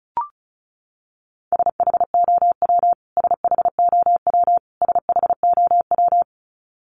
Morse code, heard as a keyed tone, about 34 wpm, 700 Hz. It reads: SHOW SHOW SHOW
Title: Top 300 Words Repeated 35wpm